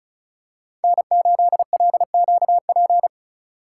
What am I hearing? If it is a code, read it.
Morse code, heard as a keyed tone, about 36 wpm, 700 Hz. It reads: N8LQP
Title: USA Call Signs - 5 Characters - 3x Character Spacing - 35wpm